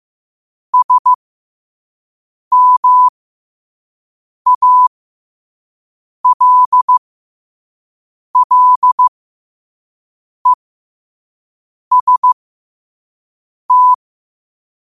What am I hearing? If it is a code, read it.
Morse code, heard as a keyed tone, about 15 wpm, spaced as for 5 wpm, 1000 Hz. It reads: SMALLEST
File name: 5wpm